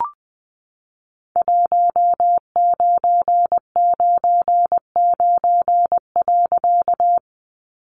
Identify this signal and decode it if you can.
Morse code, heard as a keyed tone, about 20 wpm, 700 Hz. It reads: 1999.
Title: A Sound Of Thunder 20wpm